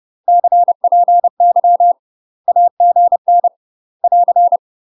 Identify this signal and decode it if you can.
Morse code, heard as a keyed tone, about 30 wpm, 700 Hz. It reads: CPY AGN <AR>
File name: UK MASTER QSOs TRAINING / 30WPM / UK MASTER QSOs 0031